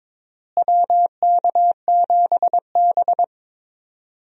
Morse code, heard as a keyed tone, about 22 wpm, 700 Hz. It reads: WK7B
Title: USA Call Signs - 4 Characters - Repeated 3x - 22wpm